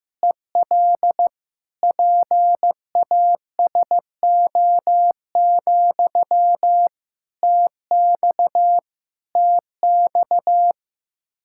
Morse code, heard as a keyed tone, about 15 wpm, 700 Hz. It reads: EL PASO, TX TX
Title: Practice QSO - Exchange 1 - Beginning - 15wpm